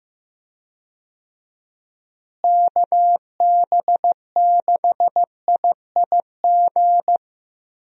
Morse code, heard as a keyed tone, about 16 wpm, 700 Hz. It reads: KB6IIG
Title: USA Call Signs - 6 Characters - Repeated 2x - 15wpm